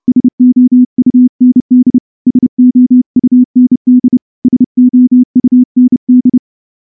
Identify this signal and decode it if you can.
Morse code, heard as a keyed tone, about 29 wpm, 250 Hz. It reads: SOUND SOUND SOUND
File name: morse-training-30WPM